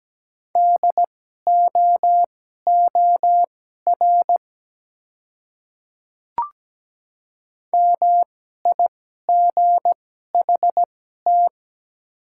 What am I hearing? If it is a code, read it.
Morse code, heard as a keyed tone, about 17 wpm, spaced as for 12 wpm, 700 Hz. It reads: DOOR MIGHT
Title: 2x Character Spacing - Top 300 Words - 17wpm